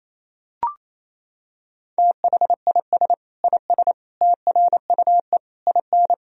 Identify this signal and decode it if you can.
Morse code, heard as a keyed tone, about 28 wpm, 700 Hz. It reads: THIS IS TRUE IN
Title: Sentences from Top 1000 Words - Review Part 3 - 28wpm